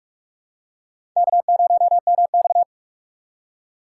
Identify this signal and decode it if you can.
Morse code, heard as a keyed tone, about 44 wpm, 700 Hz. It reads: K0MX
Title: USA Call Signs - 3x Character Spacing - 45wpm